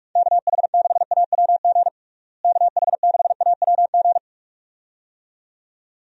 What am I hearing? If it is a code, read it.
Morse code, heard as a keyed tone, about 46 wpm, 700 Hz. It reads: KH6AWG KH6AWG
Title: DXCC-100 Call Signs - 6 Characters - Repeated 2x - 45wpm